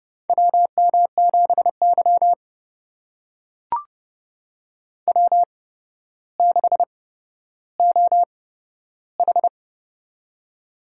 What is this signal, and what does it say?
Morse code, heard as a keyed tone, about 30 wpm, spaced as for 13 wpm, 700 Hz. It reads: WM7Y W6OH
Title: USA Call Signs - 4 Characters - 8x Character Spacing - 30wpm